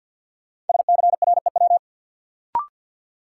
Morse code, heard as a keyed tone, about 52 wpm, 700 Hz. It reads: SCREW
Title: Sentences from Top 1600 Words - New Word Drill Alt - 50wpm